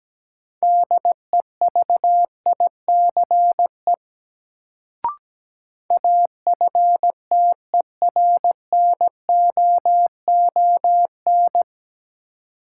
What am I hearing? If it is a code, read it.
Morse code, heard as a keyed tone, about 17 wpm, 700 Hz. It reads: DEVICE AFTERNOON
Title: Sentences from Top 1500 Words - New Word Drill Alt - 17wpm